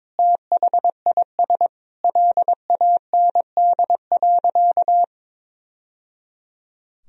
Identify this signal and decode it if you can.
Morse code, heard as a keyed tone, about 22 wpm, 700 Hz. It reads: THIS LAND.
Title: Sentences from Top 1000 Words - Review Part 4 - 22wpm